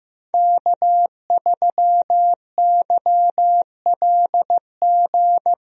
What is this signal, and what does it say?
Morse code, heard as a keyed tone, about 15 wpm, 700 Hz. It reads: K3YLG
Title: USA Call Signs - 5 Characters - Repeated 2x - 15wpm